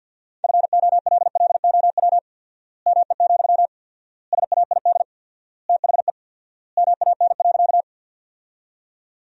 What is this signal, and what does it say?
Morse code, heard as a keyed tone, about 50 wpm, 700 Hz. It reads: FOLLOW ME, SAID THE MAN.
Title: 4x Word-Spacing - Sentences from Top 200 Words - 50wpm